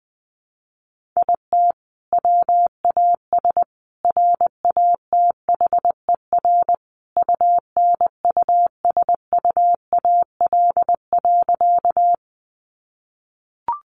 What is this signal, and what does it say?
Morse code, heard as a keyed tone, about 20 wpm, 700 Hz. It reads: IT WAS RATHER UNUSUAL.
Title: Sentences from Top 1400 Words 20wpm